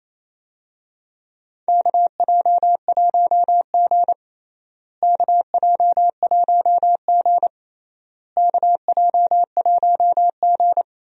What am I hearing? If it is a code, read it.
Morse code, heard as a keyed tone, about 28 wpm, 700 Hz. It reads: KJ1G KJ1G KJ1G
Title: USA Call Signs - 4 Characters - Repeated 3x - 28wpm